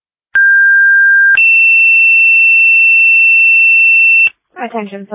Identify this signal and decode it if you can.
Two-tone page, heard as a one-second tone, then a three-second tone, about 1600 then 2700 Hz, with noise. It is heard from 0.3 to 4.3 s.